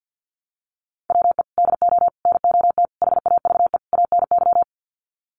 Morse code, heard as a keyed tone, about 50 wpm, 700 Hz. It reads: WE DO NOT HAVE ANY
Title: Sets of 5 Words 50wpm